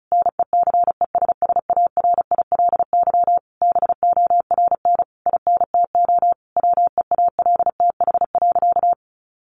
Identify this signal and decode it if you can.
Morse code, heard as a keyed tone, about 34 wpm, 700 Hz. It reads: NECESSARILY BORN INTO WEALTH.
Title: Sentences from Top 1400 Words 35wpm